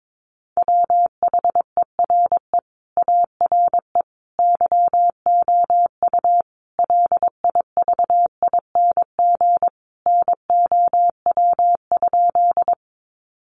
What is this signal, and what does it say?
Morse code, heard as a keyed tone, about 22 wpm, 700 Hz. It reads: WHERE ARE YOU LIVING NOW?
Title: Sentences from Top 1100 Words 22wpm